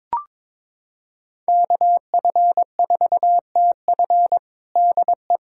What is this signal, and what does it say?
Morse code, heard as a keyed tone, about 23 wpm, 700 Hz. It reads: KF4TF DE
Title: USA Callsign DE Callsign - 5 Characters - 22wpm